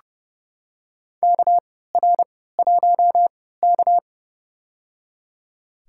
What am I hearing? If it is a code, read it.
Morse code, heard as a keyed tone, about 30 wpm, spaced as for 17 wpm, 700 Hz. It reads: KR1K
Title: USA Call Signs - 4 Characters - 3x Character Spacing - 30wpm